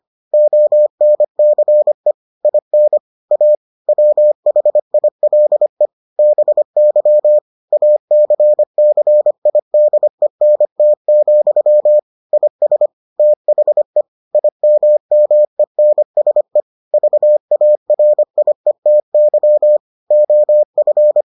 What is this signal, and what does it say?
Morse code, heard as a keyed tone, about 25 wpm, 600 Hz. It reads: ONCE IN A WHILE BY ACCIDENT, IS THE IMMENSE VARIETY OF